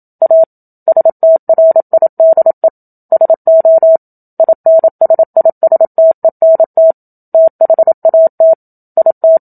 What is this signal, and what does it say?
Morse code, heard as a keyed tone, about 27 wpm, 650 Hz. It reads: A STRIDE SO INSISTENT THAT IT
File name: Book Two / 01-Chapter 1 - 27 WPM